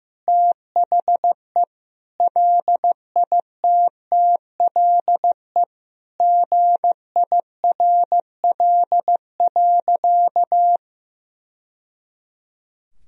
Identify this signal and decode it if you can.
Morse code, heard as a keyed tone, about 15 wpm, 700 Hz. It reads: THE LITTLE GIRL.
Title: Sentences from Top 1000 Words - Review Part 1 - 15wpm